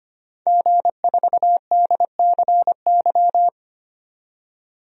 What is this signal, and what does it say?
Morse code, heard as a keyed tone, about 25 wpm, 700 Hz. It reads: G4DCY
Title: DXCC-100 Call Signs - 5 Characters - 25wpm